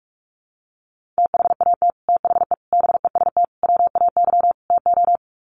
Morse code, heard as a keyed tone, about 44 wpm, 700 Hz. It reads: THAT THE BEST WAY TO
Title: Sets of 5 Words Encore2 45wpm